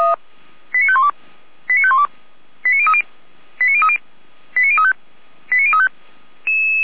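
Five-tone selective call (ZVEI), heard as quick runs of five tones, tones of about 70 ms. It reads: 89512 89512 89030 89030 89035 89035
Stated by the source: ZVEI five-tone call received by SDR.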